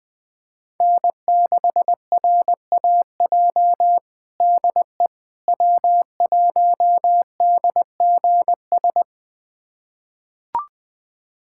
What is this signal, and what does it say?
Morse code, heard as a keyed tone, about 20 wpm, 700 Hz. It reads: N6RAJ DE W1DGS
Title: USA Callsign DE Callsign - 5 Characters - 20wpm